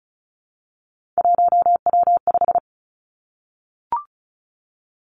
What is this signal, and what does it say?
Morse code, heard as a keyed tone, about 36 wpm, 700 Hz. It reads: 1W5
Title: Three Challenging Letter-Number Combinations 35wpm